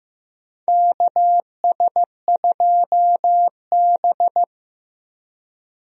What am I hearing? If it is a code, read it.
Morse code, heard as a keyed tone, about 15 wpm, 700 Hz. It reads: KS2B
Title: USA Call Signs - 4 Characters - Repeated 3x - 15wpm